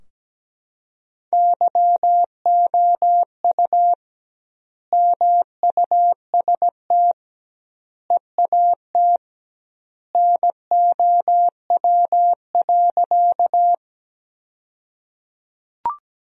Morse code, heard as a keyed tone, about 17 wpm, 700 Hz. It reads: YOU MUST EAT NOW.
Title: 2x Word-Spacing - Sentences from Top 300 Words - 17wpm